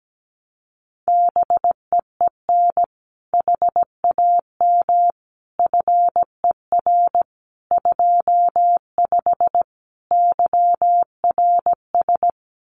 Morse code, heard as a keyed tone, about 17 wpm, 700 Hz. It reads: BEEN HAM FER 25 YRS
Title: Common QSO Phrases Encore 17wpm